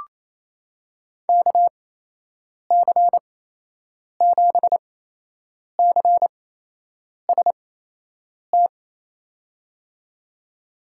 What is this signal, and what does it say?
Morse code, heard as a keyed tone, about 28 wpm, spaced as for 8 wpm, 700 Hz. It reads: KC7CST
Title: USA Call Signs - 8x Character Spacing - 28wpm